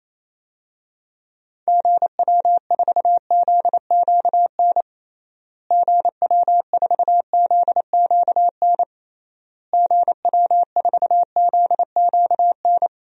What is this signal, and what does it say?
Morse code, heard as a keyed tone, about 27 wpm, 700 Hz. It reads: GW4ZQN GW4ZQN GW4ZQN
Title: DXCC-100 Call Signs - 6 Characters - Repeated 3x - 28wpm